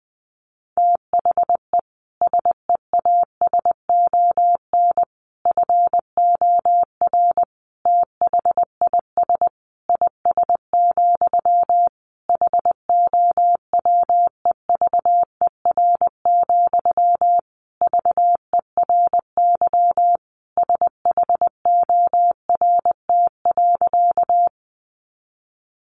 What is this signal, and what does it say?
Morse code, heard as a keyed tone, about 20 wpm, 700 Hz. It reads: THE SEASON FOR THIS IS, HOWEVER, VERY SHORT.
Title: Sentences from Top 1400 Words 20wpm